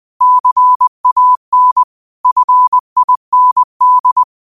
Morse code, heard as a keyed tone, about 20 wpm, 1000 Hz. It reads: CAN FIND
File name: qotdfiles / 20wpm